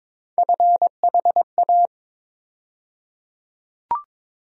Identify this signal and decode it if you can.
Morse code, heard as a keyed tone, about 22 wpm, 700 Hz. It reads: FHA
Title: USA Call Sign Postfixes - 3 Letters 22wpm